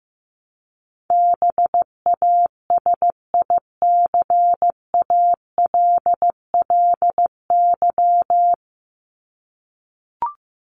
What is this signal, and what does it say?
Morse code, heard as a keyed tone, about 15 wpm, 700 Hz. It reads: BASICALLY
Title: Top 1251-1500 Words 15wpm